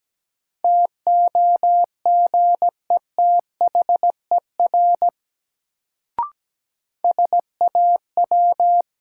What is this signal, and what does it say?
Morse code, heard as a keyed tone, about 17 wpm, 700 Hz. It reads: TOGETHER SAW
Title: Sentences from Top 300 Words - New Word Drill - 17wpm